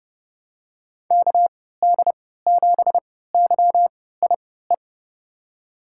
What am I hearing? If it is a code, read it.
Morse code, heard as a keyed tone, about 30 wpm, spaced as for 17 wpm, 700 Hz. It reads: KD7YIE
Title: USA Call Signs - 6 Characters - 3x Character Spacing - 30wpm